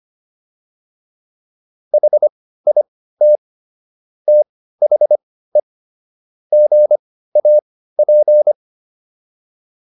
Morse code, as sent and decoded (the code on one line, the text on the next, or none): .... .. - / - .... / --. .- .--.
HIT THE GAP